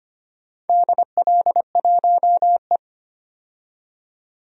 Morse code, heard as a keyed tone, about 25 wpm, 700 Hz. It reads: DL1E
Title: DXCC-100 Call Signs - 4 Characters - 25wpm